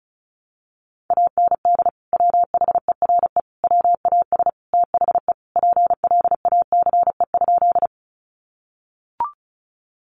Morse code, heard as a keyed tone, about 34 wpm, 700 Hz. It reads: AND WHERE WAS THE PLACE?
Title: Sentences from Top 200 Words 35wpm